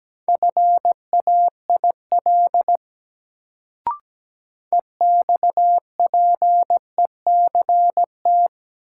Morse code, heard as a keyed tone, about 17 wpm, 700 Hz. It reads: FAIL EXPECT
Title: Sentences from Top 1100 Words - New Word Drill - 17wpm